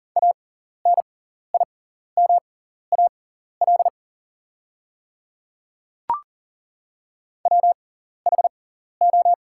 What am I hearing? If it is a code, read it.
Morse code, heard as a keyed tone, about 40 wpm, spaced as for 14 wpm, 700 Hz. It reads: ANIMAL WHO
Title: 6x Character Spacing - Top 300 Words - 40wpm